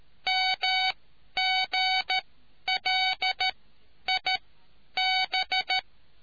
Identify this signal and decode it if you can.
Morse code, heard as a keyed tone, about 13 wpm, spaced as for 10 wpm, 750 Hz. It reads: MGLIB